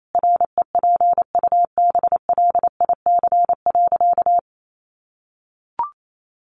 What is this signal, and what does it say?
Morse code, heard as a keyed tone, about 28 wpm, 700 Hz. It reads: REPUBLIC.
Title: Sentences from Top 1500 Words 28wpm